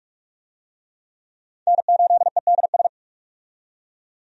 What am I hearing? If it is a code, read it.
Morse code, heard as a keyed tone, about 43 wpm, 700 Hz. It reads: N8EDS
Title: USA Call Signs - 5 Characters - 45wpm